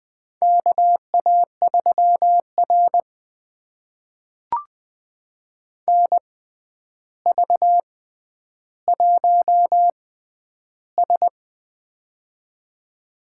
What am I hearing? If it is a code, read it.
Morse code, heard as a keyed tone, about 20 wpm, spaced as for 10 wpm, 700 Hz. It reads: KA3R NV1S
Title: USA Call Signs - 4 Characters - 6x Character Spacing - 20wpm